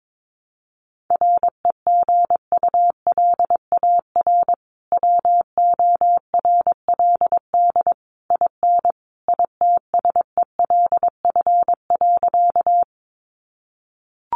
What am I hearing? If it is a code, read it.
Morse code, heard as a keyed tone, about 22 wpm, 700 Hz. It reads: REGULAR WORLD IN ITSELF.
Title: Sentences from Top 900 Words 22wpm